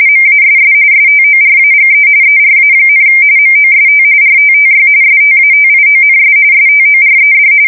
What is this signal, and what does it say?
RTTY, about 45.45 baud. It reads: MASS STORAGE ERA, MOST RTTY STATIONS STORED
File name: fsk170 / teletype-RTTY